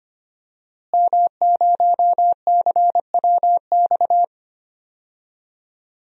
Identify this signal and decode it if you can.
Morse code, heard as a keyed tone, about 25 wpm, 700 Hz. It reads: M0CWX
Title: DXCC-100 Call Signs - 5 Characters - Repeated 3x - 25wpm